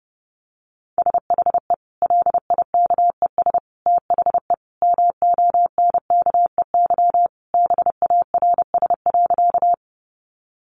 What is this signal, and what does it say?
Morse code, heard as a keyed tone, about 30 wpm, 700 Hz. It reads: SHE LIKES THE MONKEY BARS.
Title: Easy Phrases 30wpm